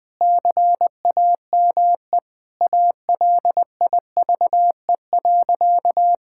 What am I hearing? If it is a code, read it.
Morse code, heard as a keyed tone, about 21 wpm, 700 Hz. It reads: CAME ALIVE.
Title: Sentences with 5-Word Phrases - 20wpm